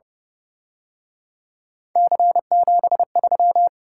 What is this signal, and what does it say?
Morse code, heard as a keyed tone, about 31 wpm, 700 Hz. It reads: C73
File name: Speed-Racing - Three Letter-Number Combinations - 25wpm